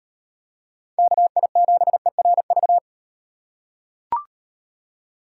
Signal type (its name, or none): Morse code